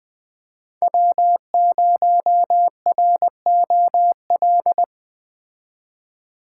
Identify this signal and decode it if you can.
Morse code, heard as a keyed tone, about 20 wpm, 700 Hz. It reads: W0ROL
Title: Speed Racing - USA 1x3 Call Signs 17wpm